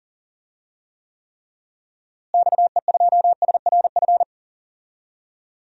Morse code, heard as a keyed tone, about 40 wpm, 700 Hz. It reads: XE2SRF
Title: DXCC-100 Call Signs - 6 Characters - Repeated 3x - 40wpm